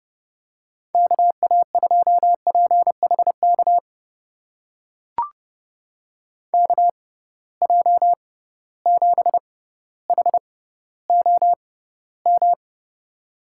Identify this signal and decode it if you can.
Morse code, heard as a keyed tone, about 32 wpm, spaced as for 16 wpm, 700 Hz. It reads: KA2PHK KJ7HOM
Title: USA Call Signs - 6 Characters - 6x Character Spacing - 30wpm